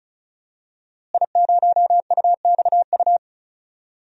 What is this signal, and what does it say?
Morse code, heard as a keyed tone, about 36 wpm, 700 Hz. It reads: I0UXU
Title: DXCC-100 Call Signs - 5 Characters - Repeated 2x - 35wpm